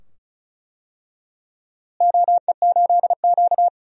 Morse code, heard as a keyed tone, about 34 wpm, 700 Hz. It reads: OE8Q
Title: DXCC-100 Call Signs - 4 Characters - Repeated 2x - 35wpm